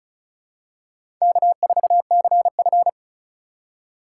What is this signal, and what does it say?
Morse code, heard as a keyed tone, about 36 wpm, 700 Hz. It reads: K4CF
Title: USA Call Signs - 3x Character Spacing - 35wpm